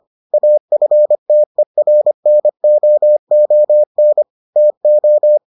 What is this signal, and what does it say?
Morse code, heard as a keyed tone, about 25 wpm, 600 Hz. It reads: AFTERNOON TO